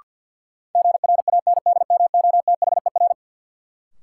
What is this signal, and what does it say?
Morse code, heard as a keyed tone, about 52 wpm, 700 Hz. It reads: GRANDMOTHER
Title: Sentences from Top 1200 Words - New Word Drill Alt - 50wpm